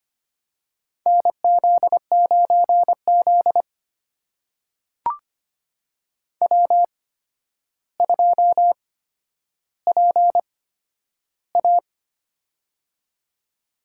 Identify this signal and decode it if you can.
Morse code, heard as a keyed tone, about 25 wpm, spaced as for 11 wpm, 700 Hz. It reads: NZ9Z W2PA
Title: USA Call Signs - 4 Characters - 8x Character Spacing - 25wpm